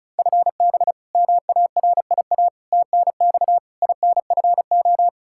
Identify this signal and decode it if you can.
Morse code, heard as a keyed tone, about 36 wpm, 700 Hz. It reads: FB MARIA TNX INFO
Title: Practice QSO - Exchange 3 - More Info - 35wpm